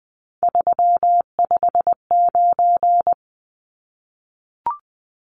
Morse code, heard as a keyed tone, about 20 wpm, 700 Hz. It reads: 359
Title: Numbers - 3-Digit 20wpm